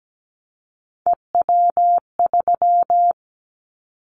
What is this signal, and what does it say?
Morse code, heard as a keyed tone, about 17 wpm, 700 Hz. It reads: EW3
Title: Callsign Prefixes for Top 50 DXCC - 17wpm